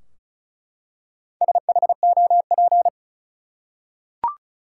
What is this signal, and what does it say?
Morse code, heard as a keyed tone, about 36 wpm, 700 Hz. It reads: SHOP
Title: Four Letter Words - 35wpm